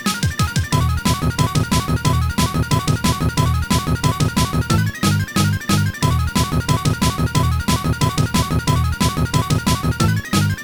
There is no code.